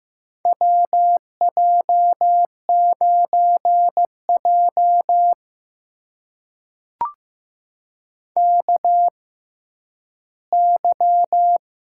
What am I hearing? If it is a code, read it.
Morse code, heard as a keyed tone, about 15 wpm, spaced as for 10 wpm, 700 Hz. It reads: WJ9J KY
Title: USA Call Signs - 4 Characters - 6x Character Spacing - 15wpm